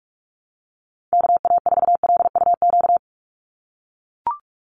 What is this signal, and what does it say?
Morse code, heard as a keyed tone, about 44 wpm, 700 Hz. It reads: KA4LUQ
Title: USA Call Signs 45wpm